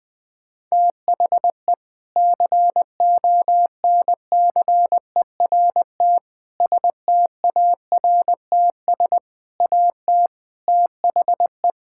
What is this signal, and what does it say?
Morse code, heard as a keyed tone, about 20 wpm, 700 Hz. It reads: THE CONCERT STARTS AT THE
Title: Sentences with 6-Word Phrases - 20wpm